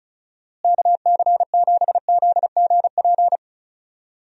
Morse code, heard as a keyed tone, about 34 wpm, 700 Hz. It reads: KC7ZGP
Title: USA Call Signs - 6 Characters - Repeated 2x - 35wpm